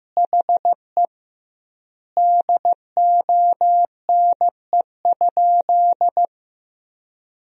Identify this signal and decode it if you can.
Morse code, heard as a keyed tone, about 15 wpm, 700 Hz. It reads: HE DONE?
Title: 2x Word-Spacing - Sentences from Top 500 Words - 15wpm